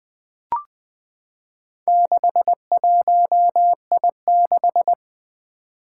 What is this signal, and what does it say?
Morse code, heard as a keyed tone, about 21 wpm, 700 Hz. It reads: 61I6
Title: Four Challenging Letter-Number Combinations 20wpm